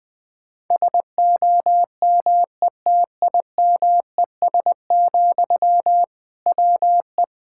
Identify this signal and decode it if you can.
Morse code, heard as a keyed tone, about 20 wpm, 700 Hz. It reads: SOMETIMES, WE
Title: Sentences from Top 1000 Words - Review Part 5 - 20wpm